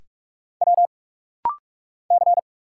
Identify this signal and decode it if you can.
Morse code, heard as a keyed tone, about 44 wpm, 700 Hz. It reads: W C
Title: Single Letter-Number Alt - Rapid-Fire 45wpm